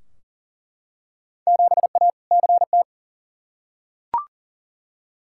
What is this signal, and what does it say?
Morse code, heard as a keyed tone, about 40 wpm, 700 Hz. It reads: 7A CT
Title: Field Day Exchanges - 40wpm